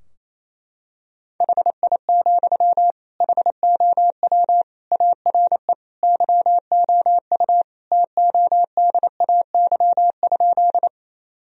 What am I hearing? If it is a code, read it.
Morse code, heard as a keyed tone, about 28 wpm, 700 Hz. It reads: HI, HOW ARE YOU TODAY?
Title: Sentences from Top 1000 Words - Review Part 3 - 28wpm